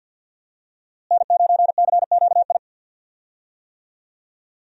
Morse code, heard as a keyed tone, about 48 wpm, 700 Hz. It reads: N9CQI
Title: USA Call Signs - 5 Characters - Repeated 2x - 50wpm